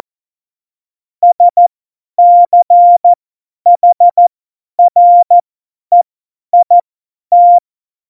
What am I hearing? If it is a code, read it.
Morse code, heard as a keyed tone, about 14 wpm, spaced as for 10 wpm, 700 Hz. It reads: SCHREIT
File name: Morsekurs / Witze / Witz 18 August 2020, 50 F70